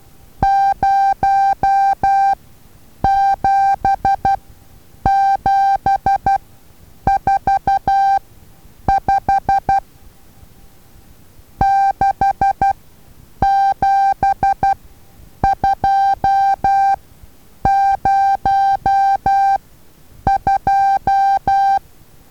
Morse code, heard as a keyed tone, about 12 wpm, spaced as for 8 wpm, 790 Hz. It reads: 07745 67202
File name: Taviro HA6VA A